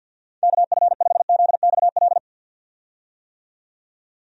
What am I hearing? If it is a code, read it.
Morse code, heard as a keyed tone, about 51 wpm, 700 Hz. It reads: KF5ZXL
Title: USA Call Signs - 6 Characters - 50wpm